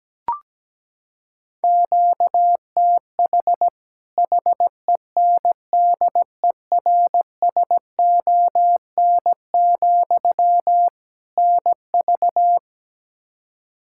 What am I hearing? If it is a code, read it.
Morse code, heard as a keyed tone, about 17 wpm, 700 Hz. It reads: QTH HENDERSON, NV
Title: Practice QSO - Exchange 1 - Beginning - 17wpm